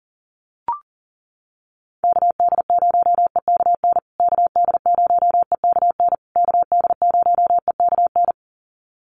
Morse code, heard as a keyed tone, about 40 wpm, 700 Hz. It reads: KD0EKN KD0EKN KD0EKN
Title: USA Call Signs Repeated 3x 40wpm